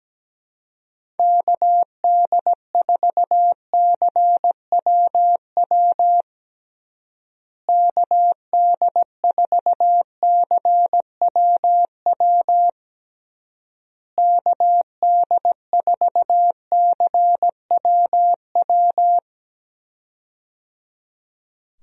Morse code, heard as a keyed tone, about 17 wpm, 700 Hz. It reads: KD4CWW KD4CWW KD4CWW
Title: USA Call Signs - 6 Characters - Repeated 3x - 17wpm